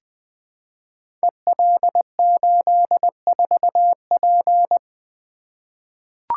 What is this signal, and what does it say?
Morse code, heard as a keyed tone, about 20 wpm, 700 Hz. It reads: EL84P
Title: Code Groups - 5 Letter:Numbers - 20wpm